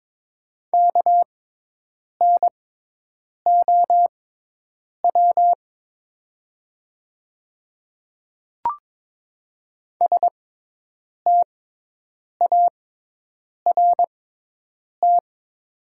Morse code, heard as a keyed tone, about 23 wpm, spaced as for 8 wpm, 700 Hz. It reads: KNOW START
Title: 6x Character Spacing - Top 500 Words - 22wpm